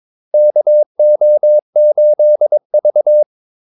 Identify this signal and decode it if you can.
Morse code, heard as a keyed tone, about 22 wpm, 600 Hz. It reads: KO8V
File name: Callsigns-22wpm